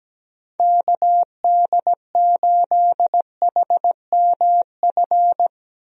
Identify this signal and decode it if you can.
Morse code, heard as a keyed tone, about 17 wpm, 700 Hz. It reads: KD8HMF
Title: USA Call Signs - 6 Characters - Repeated 2x - 17wpm